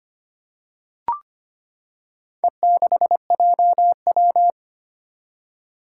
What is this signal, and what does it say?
Morse code, heard as a keyed tone, about 25 wpm, 700 Hz. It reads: E6JW
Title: Four Challenging Letter-Number Combinations 25wpm